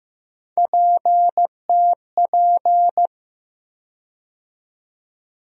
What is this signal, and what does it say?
Morse code, heard as a keyed tone, about 15 wpm, 700 Hz. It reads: PTP
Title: USA Call Sign Postfixes - 3 Letters 15wpm